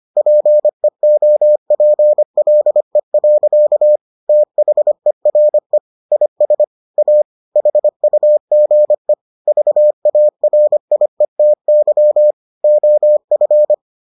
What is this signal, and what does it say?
Morse code, heard as a keyed tone, about 25 wpm, 600 Hz. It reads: PEOPLE. THERE IS A HUGE VARIETY OF